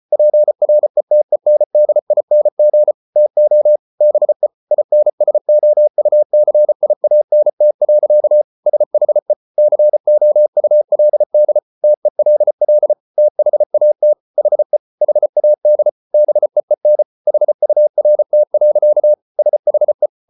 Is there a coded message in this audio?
Yes: Morse code from 0.1 to 20.1 s, heard as a keyed tone, about 34 wpm, 600 Hz. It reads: PRETENDING TO BE INSOUCIANT. SHE COULD TELL THAT HE HAD BEEN HURT. SHE